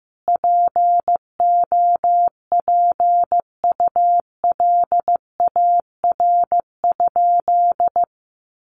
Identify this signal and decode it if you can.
Morse code, heard as a keyed tone, about 15 wpm, 700 Hz. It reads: POPULAR?
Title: Sentences from Top 1200 Words 15wpm